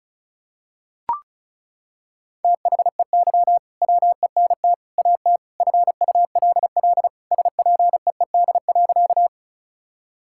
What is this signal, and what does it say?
Morse code, heard as a keyed tone, about 34 wpm, 700 Hz. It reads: THEY WENT AT FULL SPEED.
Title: Sentences from Top 1000 Words - Review Part 5 - 35wpm